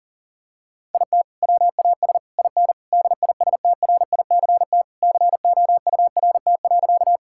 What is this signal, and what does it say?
Morse code, heard as a keyed tone, about 37 wpm, 700 Hz. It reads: IT WAS IN DISTRICT COURT.
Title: Sentences from Top 1600 Words 40wpm